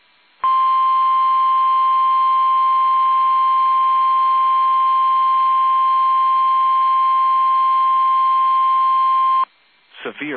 Emergency Alert System (SAME) tones (the 1050 Hz attention tone). It is heard from 0.4 to 9.5 s.
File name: eas / KUSO / KUSO